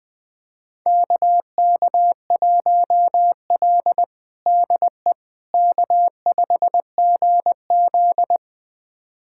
Morse code, heard as a keyed tone, about 20 wpm, 700 Hz. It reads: KK1L DE K5GZ
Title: USA Callsign DE Callsign - 4 Characters - 20wpm